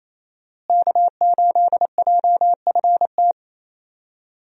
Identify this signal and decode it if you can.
Morse code, heard as a keyed tone, about 28 wpm, 700 Hz. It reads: K8JFT
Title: USA Call Signs - 5 Characters - Repeated 2x - 28wpm